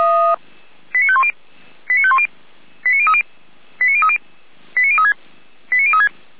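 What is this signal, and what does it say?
Five-tone selective call (ZVEI), heard as quick runs of five tones, tones of about 70 ms. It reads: 89510 89510 89030 89030 89036 89036
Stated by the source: Five-tone ZVEI selective call received by SDR.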